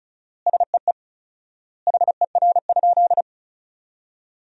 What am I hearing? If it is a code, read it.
Morse code, heard as a keyed tone, about 34 wpm, 700 Hz. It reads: SEE HER?
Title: 4x Word-Spacing - Sentences from Top 100 Words - 35wpm